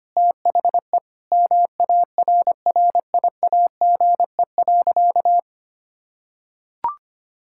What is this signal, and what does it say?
Morse code, heard as a keyed tone, about 25 wpm, 700 Hz. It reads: THE MARRIAGE.
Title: Sentences from Top 1000 Words - Review Part 6 - 25wpm